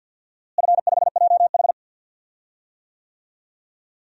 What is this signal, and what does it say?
Morse code, heard as a keyed tone, about 49 wpm, 700 Hz. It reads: F5JH
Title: DXCC-100 Call Signs - 4 Characters - Repeated 2x - 50wpm